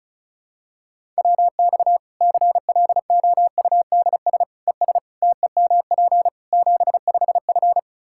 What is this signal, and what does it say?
Morse code, heard as a keyed tone, about 36 wpm, 700 Hz. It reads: WX CLOUDS ES TEMP 75F
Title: Practice QSO - Exchange 2 - Info - 35wpm